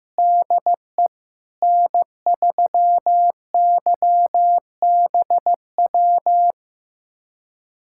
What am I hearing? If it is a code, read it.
Morse code, heard as a keyed tone, about 15 wpm, 700 Hz. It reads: DE N3YBW
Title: USA Callsign DE Callsign 15wpm